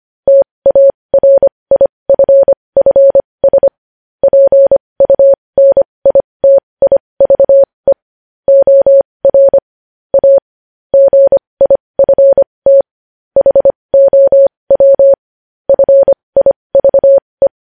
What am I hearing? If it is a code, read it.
Morse code, heard as a keyed tone, about 25 wpm, spaced as for 20 wpm, 550 Hz. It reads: TARIFFS PUNITIVE OR A GIFT HOW FIVE
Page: news headlines in morse code at 20 wpm